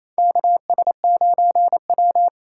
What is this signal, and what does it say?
Morse code, heard as a keyed tone, about 28 wpm, 700 Hz. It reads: KS9W
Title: K1USN SST QSO Practice - 28wpm